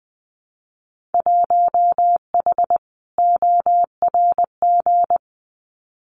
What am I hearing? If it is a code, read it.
Morse code, heard as a keyed tone, about 20 wpm, 700 Hz. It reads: 1H ORG
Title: Winter Field Day Exchanges 20wpm